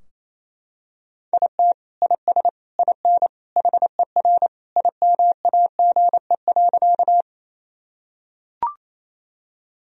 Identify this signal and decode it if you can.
Morse code, heard as a keyed tone, about 28 wpm, 700 Hz. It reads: IT IS IN HER IMAGE.
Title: Sentences from Top 1000 Words - Review Part 4 - 28wpm